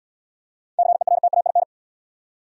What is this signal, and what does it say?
Morse code, heard as a keyed tone, about 63 wpm, 700 Hz. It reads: CERTAIN